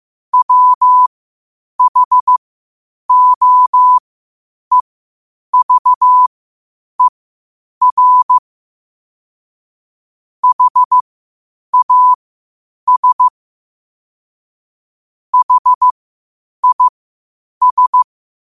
Morse code, heard as a keyed tone, about 15 wpm, spaced as for 8 wpm, 1000 Hz. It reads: WHOEVER HAS HIS
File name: qotdfiles / 8wpm